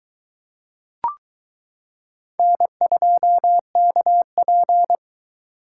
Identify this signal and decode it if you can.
Morse code, heard as a keyed tone, about 22 wpm, 700 Hz. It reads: N2KP